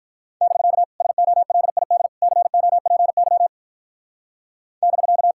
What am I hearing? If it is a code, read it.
Morse code, heard as a keyed tone, about 43 wpm, 700 Hz. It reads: <BK> SOLID COPY <BK>
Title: Speed-Racing Common QSO Phrases Encore - 35wpm